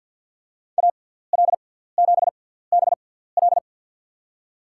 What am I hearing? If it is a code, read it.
Morse code, heard as a keyed tone, about 53 wpm, spaced as for 17 wpm, 700 Hz. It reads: AL7BL
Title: USA Call Signs - 5 Characters - 6x Character Spacing - 50wpm